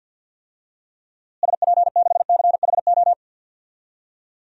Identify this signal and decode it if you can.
Morse code, heard as a keyed tone, about 49 wpm, 700 Hz. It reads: SP6CHO
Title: DXCC-100 Call Signs - 6 Characters - Repeated 3x - 50wpm